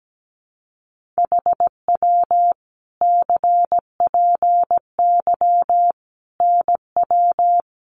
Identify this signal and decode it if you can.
Morse code, heard as a keyed tone, about 17 wpm, 700 Hz. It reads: HW CPY NW
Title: Common QSO Phrases 17wpm